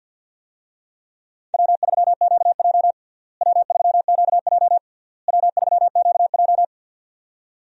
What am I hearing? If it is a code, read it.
Morse code, heard as a keyed tone, about 52 wpm, 700 Hz. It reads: W3QJ W3QJ W3QJ
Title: USA Call Signs - 4 Characters - Repeated 3x - 50wpm